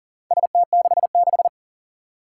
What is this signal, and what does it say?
Morse code, heard as a keyed tone, about 44 wpm, 690 Hz. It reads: ST66
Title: Four Challenging Letter-Number Combinations 40wpm